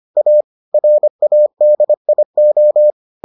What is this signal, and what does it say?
Morse code, heard as a keyed tone, about 25 wpm, 600 Hz. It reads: A RADIO